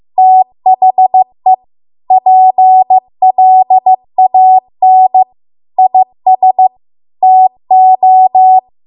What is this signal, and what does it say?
Morse code, heard as a keyed tone, about 15 wpm, 750 Hz. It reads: THE PLAN IS TO